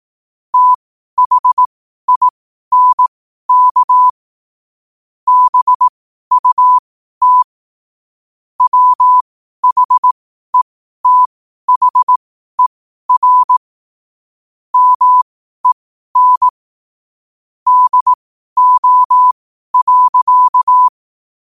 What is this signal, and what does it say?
Morse code, heard as a keyed tone, about 18 wpm, spaced as for 12 wpm, 1000 Hz. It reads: THINK BUT WHETHER MEN DO.
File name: qotdfiles / 12wpm